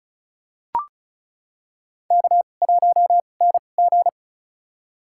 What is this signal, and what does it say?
Morse code, heard as a keyed tone, about 36 wpm, spaced as for 25 wpm, 700 Hz. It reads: K1NG
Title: USA Call Signs - 4 Characters - 2x Character Spacing - 35wpm